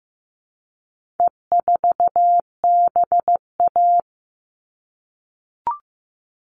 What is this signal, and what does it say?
Morse code, heard as a keyed tone, about 15 wpm, 700 Hz. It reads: E4BA
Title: All Cut Numbers - 4-Digit 15wpm